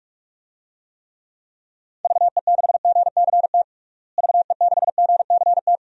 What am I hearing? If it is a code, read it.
Morse code, heard as a keyed tone, about 46 wpm, 700 Hz. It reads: VE6GCT VE6GCT
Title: DXCC-100 Call Signs - 6 Characters - Repeated 2x - 45wpm